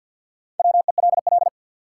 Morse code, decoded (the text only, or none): WELL